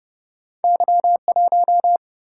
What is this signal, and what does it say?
Morse code, heard as a keyed tone, about 30 wpm, 700 Hz. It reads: Y 1
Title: Single-Letter Single-Number 30wpm